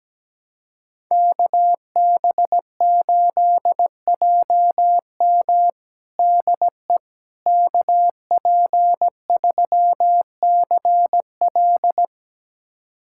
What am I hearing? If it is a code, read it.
Morse code, heard as a keyed tone, about 17 wpm, 700 Hz. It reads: KB8JM DE KP3CL
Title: USA Callsign DE Callsign - 5 Characters - 17wpm